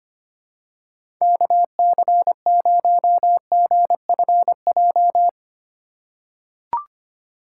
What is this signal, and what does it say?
Morse code, heard as a keyed tone, about 25 wpm, 700 Hz. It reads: KC0GFJ
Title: USA Call Signs - 6 Characters - 4x Character Spacing - 25wpm